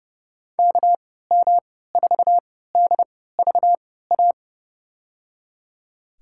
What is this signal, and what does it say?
Morse code, heard as a keyed tone, about 30 wpm, spaced as for 17 wpm, 700 Hz. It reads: KM4DVA
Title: USA Call Signs - 6 Characters - 3x Character Spacing - 30wpm